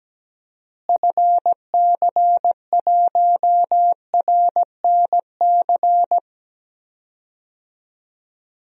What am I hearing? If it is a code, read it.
Morse code, heard as a keyed tone, about 17 wpm, 700 Hz. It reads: FC1RNC
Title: DXCC-100 Call Signs - 6 Characters - Repeated 3x - 17wpm